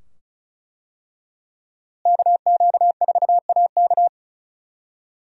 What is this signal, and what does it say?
Morse code, heard as a keyed tone, about 34 wpm, 700 Hz. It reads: KQ4AK
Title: USA Call Signs - 5 Characters - Repeated 2x - 35wpm